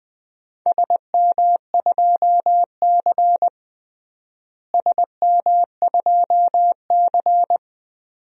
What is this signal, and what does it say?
Morse code, heard as a keyed tone, about 20 wpm, 700 Hz. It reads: SM2C SM2C
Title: DXCC-100 Call Signs - 4 Characters - Repeated 2x - 20wpm